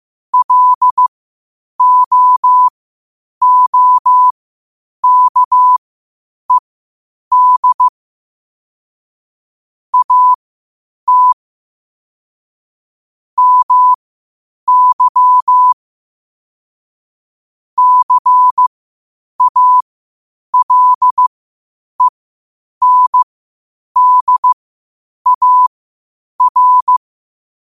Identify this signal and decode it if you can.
Morse code, heard as a keyed tone, about 15 wpm, spaced as for 8 wpm, 1000 Hz. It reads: LOOKED AT MY CALENDAR
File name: qotdfiles / 8wpm